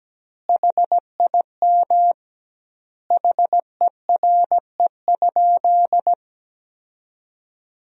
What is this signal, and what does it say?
Morse code, heard as a keyed tone, about 17 wpm, 700 Hz. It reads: HIM HERE?
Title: 2x Word-Spacing - Sentences from Top 500 Words - 17wpm